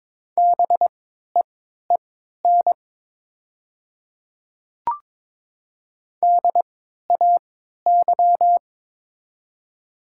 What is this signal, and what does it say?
Morse code, heard as a keyed tone, about 22 wpm, spaced as for 12 wpm, 700 Hz. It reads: BEEN DAY
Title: 3x Character Spacing - Top 100 Words - 22wpm